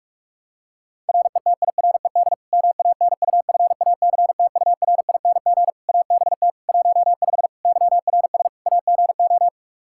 Morse code, heard as a keyed tone, about 46 wpm, 700 Hz. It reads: RETIRED MANUFACTURING ABT 15 YRS AGO
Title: Practice QSO - Exchange 3 - More Info - 45wpm